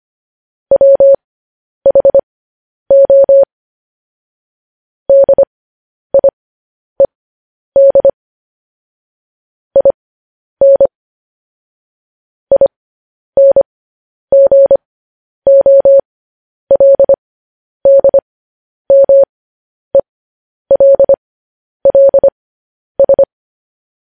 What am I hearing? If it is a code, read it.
Morse code, heard as a keyed tone, about 25 wpm, spaced as for 10 wpm, 550 Hz. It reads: WHO DIED IN INGOLDMELLS
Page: news headlines in morse code at 10 wpm